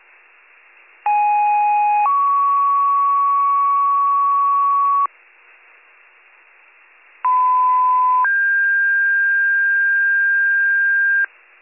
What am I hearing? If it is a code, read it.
Two-tone page, heard as a one-second tone, then a three-second tone, about 830 then 1120 Hz, about 1010 then 1600 Hz. It is heard from 1.1 to 11.3 s.